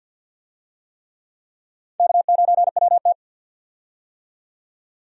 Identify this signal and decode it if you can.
Morse code, heard as a keyed tone, about 50 wpm, 700 Hz. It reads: K9WT
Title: USA Call Signs - 4 Characters - 50wpm